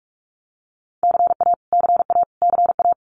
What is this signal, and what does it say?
Morse code, heard as a keyed tone, about 44 wpm, 700 Hz. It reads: CA CA CA
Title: US States Repeated 3x 45wpm